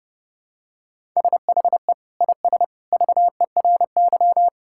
Morse code, heard as a keyed tone, about 30 wpm, 700 Hz. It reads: SHE IS VERY
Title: Sentences from Top 1000 Words - Review Part 6 - 30wpm